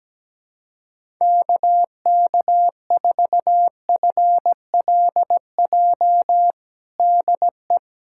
Morse code, heard as a keyed tone, about 17 wpm, 700 Hz. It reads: KK4FLJ DE
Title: USA Callsign DE Callsign - 6 Characters - 17wpm